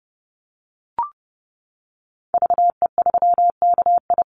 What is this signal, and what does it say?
Morse code, heard as a keyed tone, about 30 wpm, 700 Hz. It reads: VE3KI
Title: Call Signs - 30wpm